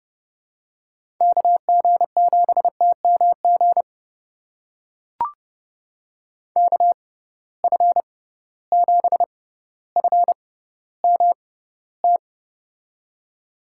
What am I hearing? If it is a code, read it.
Morse code, heard as a keyed tone, about 30 wpm, spaced as for 15 wpm, 700 Hz. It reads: KG7TMG KF7FMT
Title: USA Call Signs - 6x Character Spacing - 30wpm